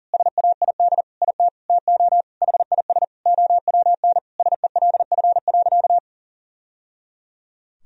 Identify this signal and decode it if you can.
Morse code, heard as a keyed tone, about 37 wpm, 700 Hz. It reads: SAID IT TO HIS OWN SELF.
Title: Sentences from Top 1000 Words - Review Part 1 - 40wpm